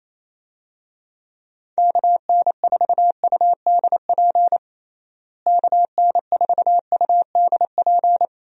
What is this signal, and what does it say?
Morse code, heard as a keyed tone, about 28 wpm, 700 Hz. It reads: KN4UDP KN4UDP
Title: USA Call Signs - 6 Characters - Repeated 3x - 28wpm